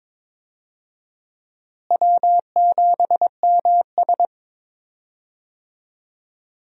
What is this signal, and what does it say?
Morse code, heard as a keyed tone, about 22 wpm, 700 Hz. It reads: W7MS
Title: USA Call Signs - 4 Characters - Repeated 3x - 22wpm